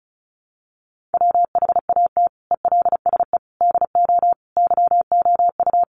Morse code, heard as a keyed tone, about 34 wpm, 700 Hz. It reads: WHAT ELSE DO YOU
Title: Sentences from Top 1100 Words 35wpm